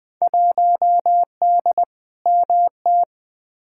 Morse code, heard as a keyed tone, about 20 wpm, 700 Hz. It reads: 1D MT
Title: Field Day Exchanges - 20wpm